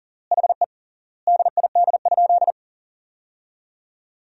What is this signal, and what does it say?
Morse code, heard as a keyed tone, about 40 wpm, 700 Hz. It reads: HE DID?
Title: 3x Word-Spacing - Sentences from Top 100 Words - 40wpm